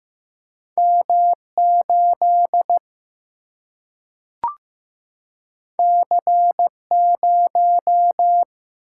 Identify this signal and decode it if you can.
Morse code, heard as a keyed tone, about 15 wpm, 700 Hz. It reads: M 8 C 0
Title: Single-Letter Single-Number 15wpm